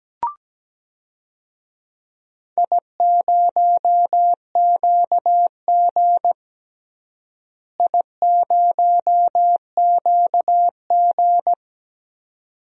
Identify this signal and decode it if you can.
Morse code, heard as a keyed tone, about 17 wpm, 700 Hz. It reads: I0QG I0QG
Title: DXCC-100 Call Signs - 4 Characters - Repeated 2x - 17wpm